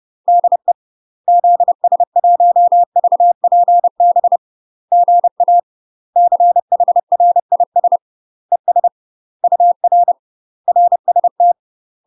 Morse code, heard as a keyed tone, about 30 wpm, 700 Hz. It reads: DE ZS1VPB GA CHRIS ES UR RST
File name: UK MASTER QSOs TRAINING / 30WPM / UK MASTER QSOs 0024